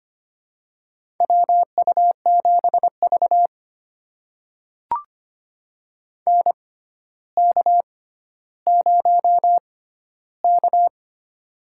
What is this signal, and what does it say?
Morse code, heard as a keyed tone, about 25 wpm, spaced as for 13 wpm, 700 Hz. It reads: WU7V NK0K
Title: USA Call Signs - 6x Character Spacing - 25wpm